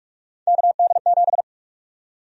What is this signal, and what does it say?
Morse code, heard as a keyed tone, about 46 wpm, 700 Hz. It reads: KD7
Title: USA Call Sign Prefixes - 45wpm